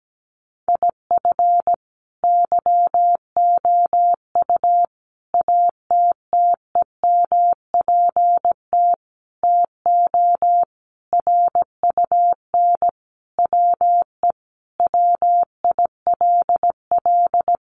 Morse code, heard as a keyed tone, about 17 wpm, 700 Hz. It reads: IF YOU ATTEMPT TO RUN WE WILL
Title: Sentences from Top 1200 Words 17wpm